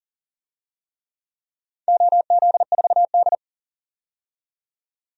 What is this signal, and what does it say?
Morse code, heard as a keyed tone, about 40 wpm, 700 Hz. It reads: OZ4D
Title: DXCC-100 Call Signs - 4 Characters - 40wpm